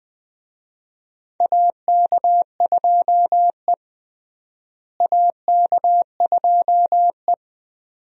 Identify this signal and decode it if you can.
Morse code, heard as a keyed tone, about 20 wpm, 700 Hz. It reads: AK2E AK2E
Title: USA Call Signs - 4 Characters - Repeated 2x - 20wpm